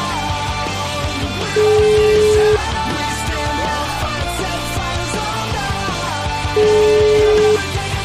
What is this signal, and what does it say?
Ringback tone (Europe, 425 Hz), with noise.